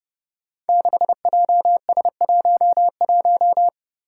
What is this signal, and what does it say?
Morse code, heard as a keyed tone, about 30 wpm, 700 Hz. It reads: 6JS11
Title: Five Challenging Letter-Number Combinations 30wpm